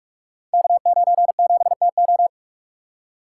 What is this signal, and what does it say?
Morse code, heard as a keyed tone, about 44 wpm, 690 Hz. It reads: K97TO
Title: Code Groups - 5 Letter:Numbers - 45wpm